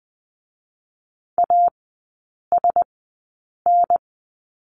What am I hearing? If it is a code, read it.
Morse code, heard as a keyed tone, about 20 wpm, spaced as for 8 wpm, 700 Hz. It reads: ASN
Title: 041 Lesson 8e - Number 1 - Three Characters 20wpm